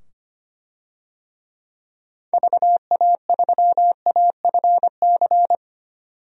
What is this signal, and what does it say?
Morse code, heard as a keyed tone, about 25 wpm, 700 Hz. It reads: VA3AFC
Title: DXCC-100 Call Signs - 6 Characters - 25wpm